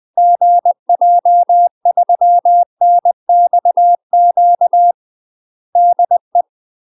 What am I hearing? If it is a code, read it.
Morse code, heard as a keyed tone, about 20 wpm, 700 Hz. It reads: GJ3NXQ DE
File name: UK MASTER QSOs TRAINING / 20WPM / UK MASTER QSOs 0002